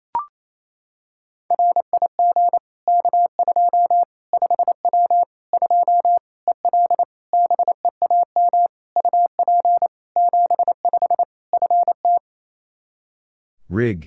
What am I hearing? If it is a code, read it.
Morse code, heard as a keyed tone, about 28 wpm, 700 Hz. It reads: RIG K2 5W 2 EL BEAM UP 75 FT
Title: Practice QSO - Exchange 2 - Info - 28wpm